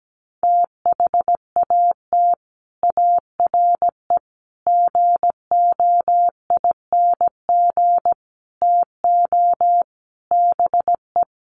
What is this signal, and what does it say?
Morse code, heard as a keyed tone, about 17 wpm, 700 Hz. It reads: THAT ARE GOING TO BE
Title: Sets of 5 Words Encore2 17wpm